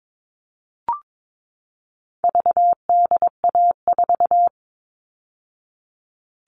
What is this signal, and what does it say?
Morse code, heard as a keyed tone, about 22 wpm, 700 Hz. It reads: VDA4
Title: All Cut Numbers - 4-Digit 22wpm